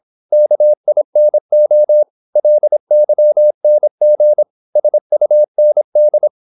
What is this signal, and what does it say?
Morse code, heard as a keyed tone, about 26 wpm, 600 Hz. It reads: KINO LYNG SUND
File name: Korte ord DK 26wpm